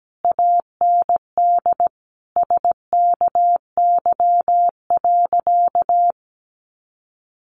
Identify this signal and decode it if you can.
Morse code, heard as a keyed tone, about 17 wpm, 700 Hz. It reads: AND SKY.
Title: Sentences from Top 1200 Words 17wpm